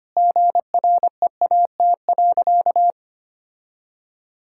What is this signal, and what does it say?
Morse code, heard as a keyed tone, about 25 wpm, 700 Hz. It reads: GREAT.
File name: practice / 25wpm / Sentences with 2-Word Phrases - 25wpm